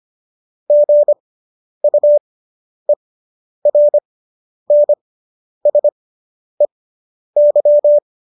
Morse code, heard as a keyed tone, about 25 wpm, spaced as for 10 wpm, 590 Hz. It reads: GUERNSEY